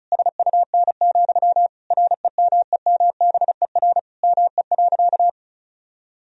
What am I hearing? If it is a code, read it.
Morse code, heard as a keyed tone, about 34 wpm, 700 Hz. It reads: SUN, REMEMBER ME.
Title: Sentences from Top 1000 Words - Review Part 1 - 35wpm